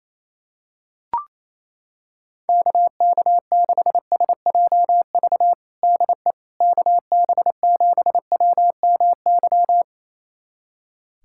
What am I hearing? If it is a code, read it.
Morse code, heard as a keyed tone, about 28 wpm, 700 Hz. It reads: KK6SJV DE KB7WMY
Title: USA Callsign DE Callsign - 6 Characters - 28wpm